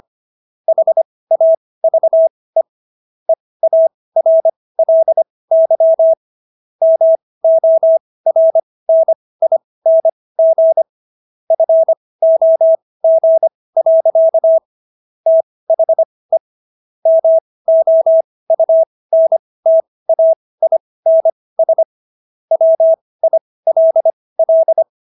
Morse code, heard as a keyed tone, about 25 wpm, spaced as for 18 wpm, 650 Hz. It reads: HAVE EARLY MORNING FOG. THE MOUNTAINS WILL